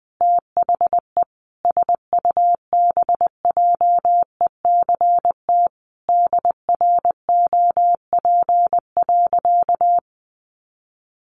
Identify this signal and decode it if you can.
Morse code, heard as a keyed tone, about 20 wpm, 700 Hz. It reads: THE SUBJECT DROP.
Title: Sentences from Top 1400 Words 20wpm